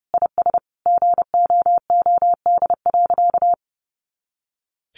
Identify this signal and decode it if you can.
Morse code, heard as a keyed tone, about 30 wpm, 700 Hz. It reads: IS GOOD.
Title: Easy Phrases 30wpm